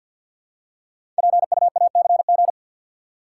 Morse code, heard as a keyed tone, about 53 wpm, 700 Hz. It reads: PUACG
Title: Code Groups - 5 Letters - 50wpm